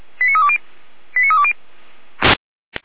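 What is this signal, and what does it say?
Five-tone selective call (ZVEI), heard as quick runs of five tones, tones of about 70 ms. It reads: 89320 89320
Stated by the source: ZVEI five-tone selective call received on an SDR, fire brigade Feuerwehr Orlamünde.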